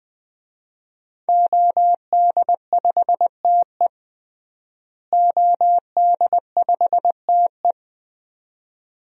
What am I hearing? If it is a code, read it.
Morse code, heard as a keyed tone, about 20 wpm, 700 Hz. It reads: OD5TE OD5TE
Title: DXCC-100 Call Signs - 5 Characters - Repeated 2x - 20wpm